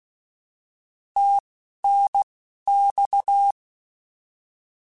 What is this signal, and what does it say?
Morse code, heard as a keyed tone, about 16 wpm, 800 Hz. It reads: T N X
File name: Abreviations / Abreviations mp3s / Abreviations@10